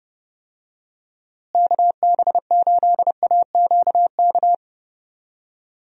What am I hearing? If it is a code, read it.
Morse code, heard as a keyed tone, about 30 wpm, 700 Hz. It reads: KB8AQK
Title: USA Call Signs - 6 Characters - Repeated 3x - 30wpm